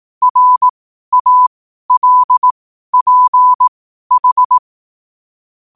Morse code, heard as a keyed tone, about 18 wpm, spaced as for 12 wpm, 1000 Hz. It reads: RALPH
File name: 12wpm